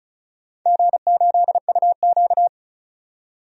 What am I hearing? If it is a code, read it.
Morse code, heard as a keyed tone, about 36 wpm, 700 Hz. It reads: G8UQ
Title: DXCC-100 Call Signs - 4 Characters - 35wpm